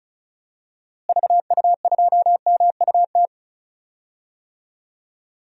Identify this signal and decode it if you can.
Morse code, heard as a keyed tone, about 34 wpm, 700 Hz. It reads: VU2MUT
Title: DXCC-100 Call Signs - 6 Characters - Repeated 2x - 35wpm